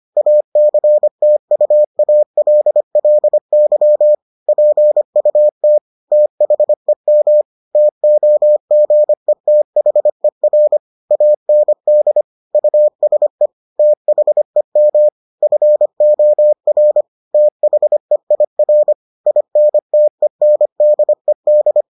Morse code, heard as a keyed tone, about 25 wpm, 600 Hz. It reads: ACTUALLY PUT THEM TOGETHER AND USE THEM FOR THEIR INTENDED